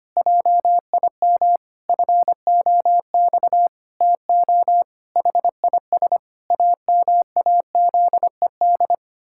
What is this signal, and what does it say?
Morse code, heard as a keyed tone, about 25 wpm, 700 Hz. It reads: JIM FOX TO HIS AMAZED
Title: Pangram Challenge v2 - 25wpm